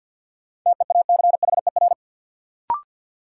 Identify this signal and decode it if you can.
Morse code, heard as a keyed tone, about 53 wpm, 700 Hz. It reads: TEACHER
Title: Sentences from Top 900 Words - New Word Drill - 50wpm